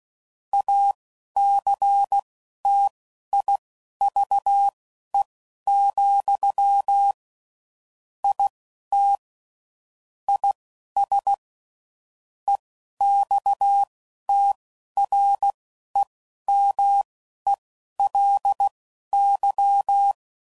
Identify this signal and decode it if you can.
Morse code, heard as a keyed tone, about 16 wpm, spaced as for 11 wpm, 800 Hz. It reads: ACTIVE, IT IS EXTREMELY
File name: CW Code Groups @10WPM / Technical Subjects / AND Gate